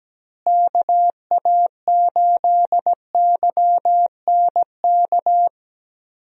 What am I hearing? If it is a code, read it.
Morse code, heard as a keyed tone, about 17 wpm, 700 Hz. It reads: KA8YNK
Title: USA Call Signs - 6 Characters - Repeated 3x - 17wpm